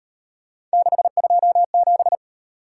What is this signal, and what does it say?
Morse code, heard as a keyed tone, about 38 wpm, 700 Hz. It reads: 627